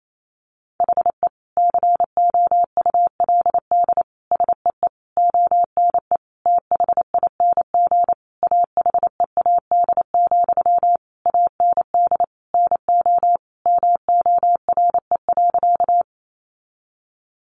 Morse code, heard as a keyed tone, about 28 wpm, 700 Hz. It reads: HE COULD SEE ONE THING AHEAD, AND NO MORE.
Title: Sentences from Top 1100 Words 28wpm